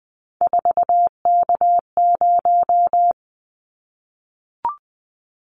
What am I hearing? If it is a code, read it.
Morse code, heard as a keyed tone, about 20 wpm, 700 Hz. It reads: 4K0
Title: Callsign Prefixes for Top 100 DXCC - 20wpm